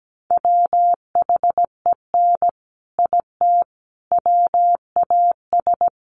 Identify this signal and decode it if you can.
Morse code, heard as a keyed tone, about 17 wpm, 700 Hz. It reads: WHEN IT WAS
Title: Sentences from Top 600 Words 17wpm